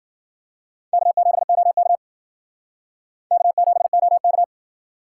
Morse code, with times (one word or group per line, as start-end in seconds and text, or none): 0.9-2.0 K7OK
3.3-4.4 K7OK